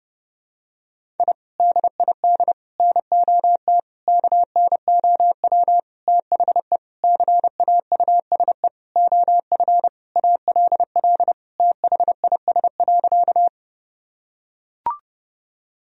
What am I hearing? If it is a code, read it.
Morse code, heard as a keyed tone, about 32 wpm, 700 Hz. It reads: I DID NOT KNOW THE CAUSE OF ALL THIS.
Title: Sentences from Top 1000 Words - Review Part 1 - 30wpm